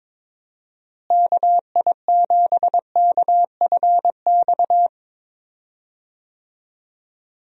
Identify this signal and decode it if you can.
Morse code, heard as a keyed tone, about 22 wpm, 700 Hz. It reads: KI7KFX
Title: USA Call Signs - 6 Characters - Repeated 3x - 22wpm